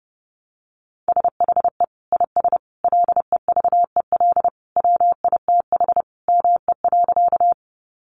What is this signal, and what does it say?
Morse code, heard as a keyed tone, about 30 wpm, 700 Hz. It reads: SHE IS LEVEL WITH ME.
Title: Sentences from Top 700 Words 30wpm